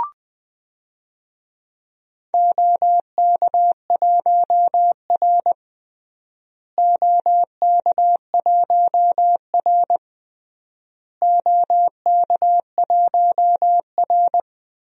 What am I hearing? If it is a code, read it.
Morse code, heard as a keyed tone, about 20 wpm, 700 Hz. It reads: OK1R OK1R OK1R
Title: DXCC-100 Call Signs - 4 Characters - Repeated 3x - 20wpm